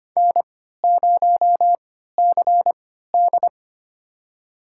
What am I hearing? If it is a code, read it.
Morse code, heard as a keyed tone, about 25 wpm, spaced as for 14 wpm, 700 Hz. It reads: N0CD
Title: USA Call Signs - 4 Characters - 3x Character Spacing - 25wpm